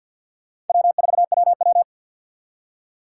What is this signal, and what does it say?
Morse code, heard as a keyed tone, about 50 wpm, 700 Hz. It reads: W4WW
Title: USA Call Signs - 3x Character Spacing - 50wpm